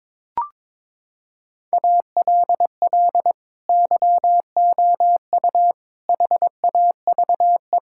Morse code, heard as a keyed tone, about 22 wpm, 700 Hz. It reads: ALL YOU HAVE
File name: Sets of 9 Words - 22wpm